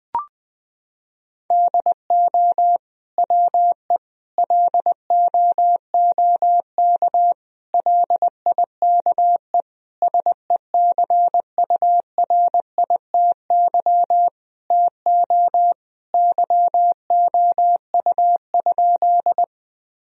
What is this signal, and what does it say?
Morse code, heard as a keyed tone, about 21 wpm, 700 Hz. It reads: DO WE LOOK LIKE SECURITY TO YOU?
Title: Sentences from Top 1000 Words - Review Part 4 - 20wpm